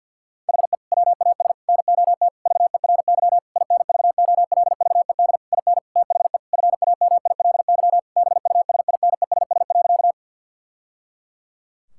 Morse code, heard as a keyed tone, about 48 wpm, 700 Hz. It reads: HE WAS NOT VERY INVOLVED IN THE FAMILY BUSINESS.
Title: Sentences from Top 1000 Words - Review Part 5 - 50wpm